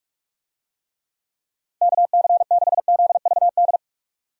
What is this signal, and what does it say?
Morse code, heard as a keyed tone, about 45 wpm, 700 Hz. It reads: KC6ZVD
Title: USA Call Signs - 6 Characters - 45wpm